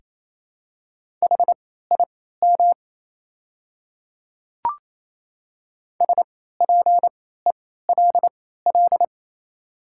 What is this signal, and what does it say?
Morse code, heard as a keyed tone, about 28 wpm, spaced as for 16 wpm, 700 Hz. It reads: HIM SPELL
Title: 3x Character Spacing - Top 300 Words - 28wpm